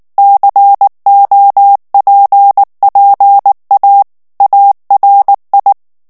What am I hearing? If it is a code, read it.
Morse code, heard as a keyed tone, about 19 wpm, 800 Hz. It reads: COPPA ARI
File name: morse mp3 / ITALIEN texte clair 2 vitesse 18 mots minute